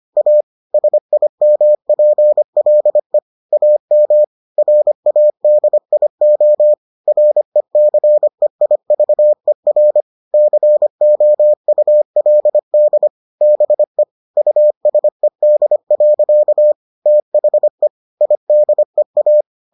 Morse code, heard as a keyed tone, about 25 wpm, 600 Hz. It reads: A SIMPLE AM RADIO RECEIVER COULD BE USED. THE IDEA